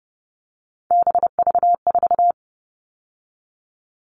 Morse code, heard as a keyed tone, about 30 wpm, 700 Hz. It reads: BV4
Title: All Cut Numbers - 3-Digit 30wpm